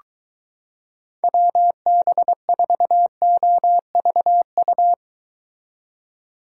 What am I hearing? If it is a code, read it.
Morse code, heard as a keyed tone, about 23 wpm, 700 Hz. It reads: WB4OVU